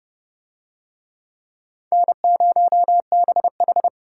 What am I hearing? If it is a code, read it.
Morse code, heard as a keyed tone, about 27 wpm, 700 Hz. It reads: N0BH